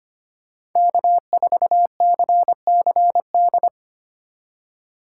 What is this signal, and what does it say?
Morse code, heard as a keyed tone, about 25 wpm, 700 Hz. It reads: K4CCD
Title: USA Call Signs - 5 Characters - 3x Character Spacing - 25wpm